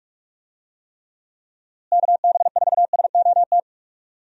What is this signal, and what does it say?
Morse code, heard as a keyed tone, about 46 wpm, 700 Hz. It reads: KB4SOT
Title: USA Call Signs - 6 Characters - Repeated 2x - 45wpm